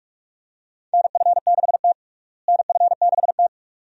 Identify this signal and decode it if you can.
Morse code, heard as a keyed tone, about 46 wpm, 700 Hz. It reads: NF6T NF6T
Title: USA Call Signs - 4 Characters - Repeated 2x - 45wpm